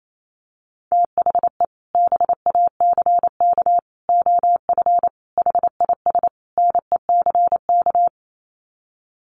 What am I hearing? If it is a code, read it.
Morse code, heard as a keyed tone, about 28 wpm, 700 Hz. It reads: THE BACK OF HIS NECK
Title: Sets of 5 Words Encore 28wpm